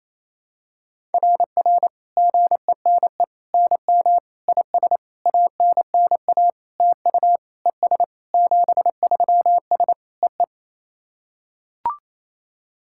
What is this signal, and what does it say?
Morse code, heard as a keyed tone, about 28 wpm, 700 Hz. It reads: RR GENE NM IS ANNA TU ES 73S EE
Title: SOTA QSO Practice 28wpm